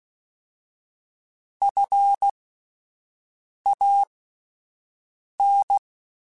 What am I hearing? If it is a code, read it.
Morse code, heard as a keyed tone, about 16 wpm, spaced as for 5 wpm, 800 Hz. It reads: FAN
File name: Alphabetical / Alphabetical mp3s / Fs @5